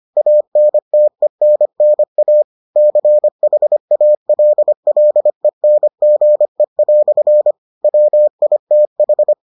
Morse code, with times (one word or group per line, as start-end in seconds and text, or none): 0.2-2.4 ANTENNA
2.8-7.5 CHALLENGE"
7.8-9.3 WITH